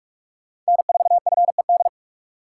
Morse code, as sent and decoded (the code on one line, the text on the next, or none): -. ....- ..-. -..
N4FED